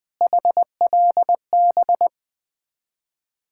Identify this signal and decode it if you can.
Morse code, heard as a keyed tone, about 20 wpm, 700 Hz. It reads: HLB
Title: Ontario QSO Party Exchange 20wpm